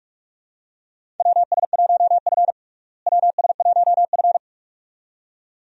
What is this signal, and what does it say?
Morse code, heard as a keyed tone, about 42 wpm, 700 Hz. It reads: WS1F WS1F
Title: USA Call Signs - 4 Characters - Repeated 2x - 45wpm